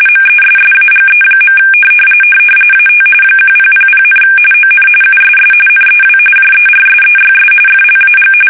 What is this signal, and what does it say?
RTTY, about 45.45 baud. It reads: MESSAGE. AT 1000, ON USS NEW JERSEYS FLAG
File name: fsk850 / battleshipnj